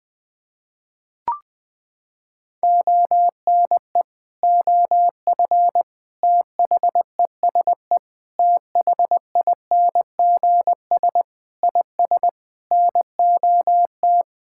Morse code, heard as a keyed tone, about 20 wpm, 700 Hz. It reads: ONE OF THESE THINGS IS NOT
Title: Sets of 8 Words - 20wpm